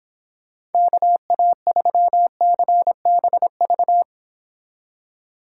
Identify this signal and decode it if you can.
Morse code, heard as a keyed tone, about 26 wpm, 700 Hz. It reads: KA3CBV